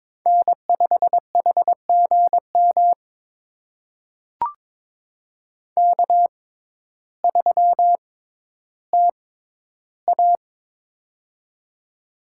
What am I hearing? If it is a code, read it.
Morse code, heard as a keyed tone, about 22 wpm, spaced as for 12 wpm, 700 Hz. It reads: N5HGM K3TA
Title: USA Call Signs - 6x Character Spacing - 22wpm